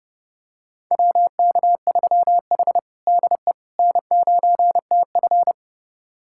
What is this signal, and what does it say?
Morse code, heard as a keyed tone, about 30 wpm, 700 Hz. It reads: WK3H DE N9TF
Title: USA Callsign DE Callsign - 4 Characters - 30wpm